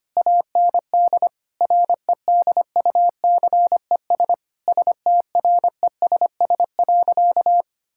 Morse code, heard as a keyed tone, about 25 wpm, 700 Hz. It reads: AND REDUCES STRESS.
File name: practice / 25wpm / Difficult Sentences with Hints - 25wpm